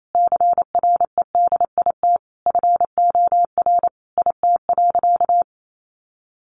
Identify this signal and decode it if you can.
Morse code, heard as a keyed tone, about 28 wpm, 700 Hz. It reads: CREDIT FOR IT.
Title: Sentences from Top 1000 Words 28wpm